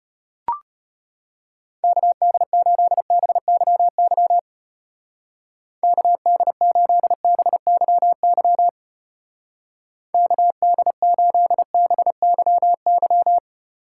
Morse code, heard as a keyed tone, about 31 wpm, 700 Hz. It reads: KD8BYY KD8BYY KD8BYY
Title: Speed Racing - USA 2x3 Call Signs 25wpm